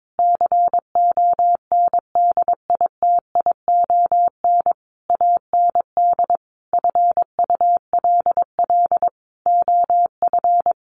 Morse code, heard as a keyed tone, about 22 wpm, 700 Hz. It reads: CONDITION AND FULL OF
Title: Sentences from Top 1500 Words 22wpm